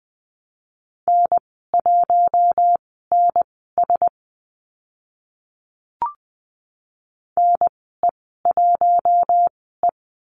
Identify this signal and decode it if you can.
Morse code, heard as a keyed tone, about 20 wpm, spaced as for 15 wpm, 700 Hz. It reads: N1NS NE1E
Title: 048 Lesson 9g - Number 4 - Call Signs 20wpm